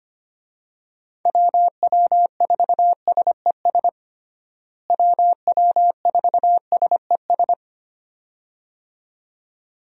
Morse code, heard as a keyed tone, about 25 wpm, 700 Hz. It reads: WW4SES WW4SES
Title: USA Call Signs - 6 Characters - Repeated 2x - 25wpm